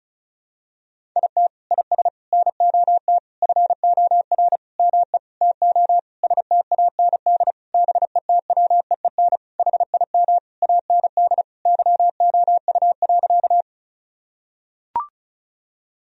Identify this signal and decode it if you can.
Morse code, heard as a keyed tone, about 36 wpm, 700 Hz. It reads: IT IS NOT FOR ME TO STAND BETWEEN HIM AND YOU.
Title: Sentences from Top 1000 Words - Review Part 1 - 35wpm